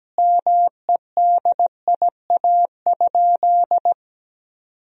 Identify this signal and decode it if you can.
Morse code, heard as a keyed tone, about 17 wpm, 700 Hz. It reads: MEDIA?
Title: Sentences from Top 1000 Words - Review Part 4 - 17wpm